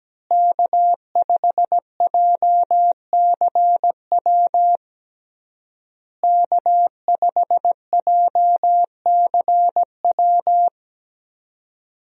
Morse code, heard as a keyed tone, about 17 wpm, 700 Hz. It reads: K5JCW K5JCW
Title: USA Call Signs - 5 Characters - Repeated 3x - 17wpm